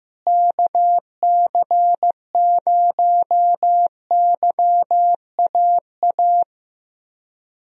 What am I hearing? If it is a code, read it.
Morse code, heard as a keyed tone, about 14 wpm, 700 Hz. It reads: KC0YAA
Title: USA Call Signs - 6 Characters - Repeated 3x - 15wpm